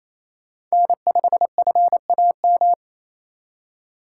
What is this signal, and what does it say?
Morse code, heard as a keyed tone, about 35 wpm, 700 Hz. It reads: N5FAM
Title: Speed Racing - USA 1x3 Call Signs 28wpm